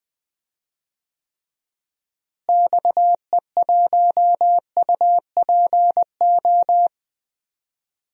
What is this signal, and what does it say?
Morse code, heard as a keyed tone, about 19 wpm, 700 Hz. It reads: XE1UPO
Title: DXCC-100 Call Signs - 6 Characters - Repeated 3x - 20wpm